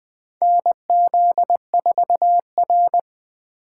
Morse code, heard as a keyed tone, about 20 wpm, 700 Hz. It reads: NZ4R
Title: 2023 SOTA Activator Call Signs - 20wpm